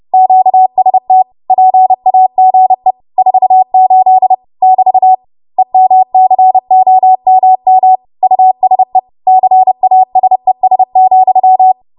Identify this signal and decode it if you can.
Morse code, heard as a keyed tone, about 30 wpm, 750 Hz. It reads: QST PAGE 48 <BT> EMCOMM USE CASES,
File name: Morse / 30WPM